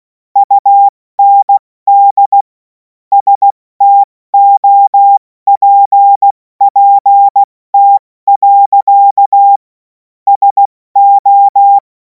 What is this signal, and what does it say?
Morse code, heard as a keyed tone, about 16 wpm, spaced as for 14 wpm, 800 Hz. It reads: UND STOPPT. SO
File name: Morsekurs / Witze / Witz 28. April 2026 70 F80